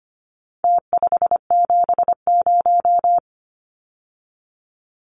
Morse code, heard as a keyed tone, about 25 wpm, 700 Hz. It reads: T570
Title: Numbers - Contest Style - 4-Digit 25wpm